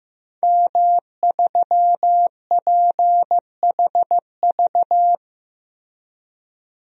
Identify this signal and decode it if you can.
Morse code, heard as a keyed tone, about 15 wpm, 700 Hz. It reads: M3PHV
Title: DXCC-100 Call Signs - 5 Characters - Repeated 2x - 15wpm